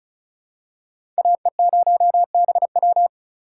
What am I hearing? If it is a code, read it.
Morse code, heard as a keyed tone, about 36 wpm, 700 Hz. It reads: AE0BW
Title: USA Call Signs - 5 Characters - 3x Character Spacing - 35wpm